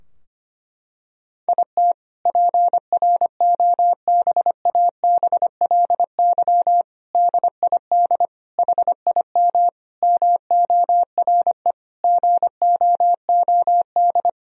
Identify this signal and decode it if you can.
Morse code, heard as a keyed tone, about 25 wpm, 700 Hz. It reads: IT PROBABLY DID HIM MORE GOOD
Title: Sentences from Top 1000 Words - Review Part 4 - 25wpm